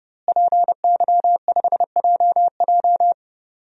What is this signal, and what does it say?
Morse code, heard as a keyed tone, about 30 wpm, 700 Hz. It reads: PY5JJ
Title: DXCC-100 Call Signs - 5 Characters - Repeated 2x - 30wpm